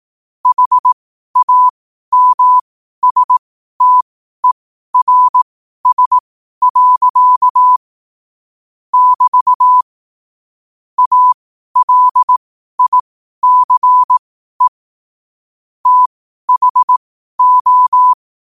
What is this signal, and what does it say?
Morse code, heard as a keyed tone, about 18 wpm, spaced as for 12 wpm, 1000 Hz. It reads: HAMSTERS. <BT> ALICE THO
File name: qotdfiles / 12wpm